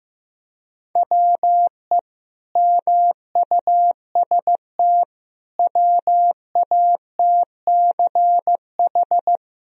Morse code, heard as a keyed tone, about 16 wpm, 700 Hz. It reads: WE MUST WATCH
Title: Sentences from Top 1000 Words - Review Part 1 - 15wpm